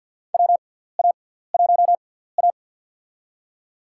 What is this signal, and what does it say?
Morse code, heard as a keyed tone, about 53 wpm, spaced as for 17 wpm, 700 Hz. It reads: WA1A
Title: USA Call Signs - 4 Characters - 6x Character Spacing - 50wpm